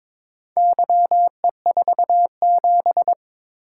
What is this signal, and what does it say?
Morse code, heard as a keyed tone, about 22 wpm, 700 Hz. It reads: YE47
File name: practice / 22wpm / Four Letter-Number Combinations - 22wpm